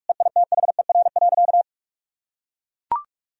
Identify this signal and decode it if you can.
Morse code, heard as a keyed tone, about 41 wpm, 700 Hz. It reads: EITHER.
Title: Sentences from Top 1000 Words - Review Part 4 - 45wpm